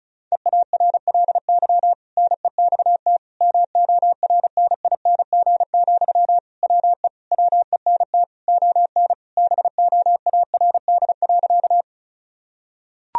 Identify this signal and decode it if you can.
Morse code, heard as a keyed tone, about 36 wpm, 700 Hz. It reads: EARLY NEXT MORNING, WE WENT ON BOARD.
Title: Sentences from Top 1000 Words - Review Part 4 - 35wpm